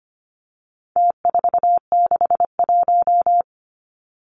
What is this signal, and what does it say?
Morse code, heard as a keyed tone, about 25 wpm, 700 Hz. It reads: T461
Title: Numbers - Contest Style - 4-Digit 25wpm